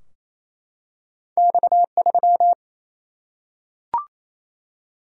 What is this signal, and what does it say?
Morse code, heard as a keyed tone, about 27 wpm, 700 Hz. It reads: X 3
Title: Single-Letter Single-Number 28wpm